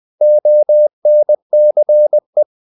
Morse code, heard as a keyed tone, about 20 wpm, 600 Hz. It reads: ONCE
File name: the old man and the sea part 2 20wpm